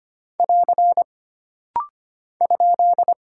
Morse code, heard as a keyed tone, about 25 wpm, 700 Hz. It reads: <AR> ?
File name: Single Character (Letter:Number:Punctuation:Prosign) - Rapid-Fire - 25wpm